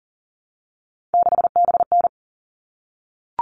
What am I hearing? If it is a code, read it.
Morse code, heard as a keyed tone, about 40 wpm, 700 Hz. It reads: 6BN
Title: All Cut Numbers - 3-Digit 40wpm